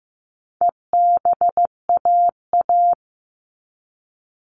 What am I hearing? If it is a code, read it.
Morse code, heard as a keyed tone, about 15 wpm, 700 Hz. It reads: EBAA
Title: All Cut Numbers - 4-Digit 15wpm